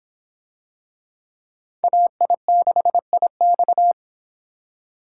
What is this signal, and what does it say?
Morse code, heard as a keyed tone, about 23 wpm, 700 Hz. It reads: AI6IX